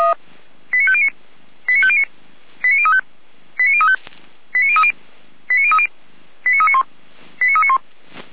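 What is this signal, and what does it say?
Five-tone selective call (ZVEI), heard as quick runs of five tones, tones of about 70 ms. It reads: 89509 89509 89035 89035 89030 89030 89391 89391
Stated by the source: ZVEI five-tone selective call received on SDR.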